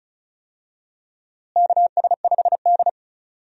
Morse code, heard as a keyed tone, about 36 wpm, 700 Hz. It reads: KS5D
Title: USA Call Signs - 4 Characters - Repeated 3x - 35wpm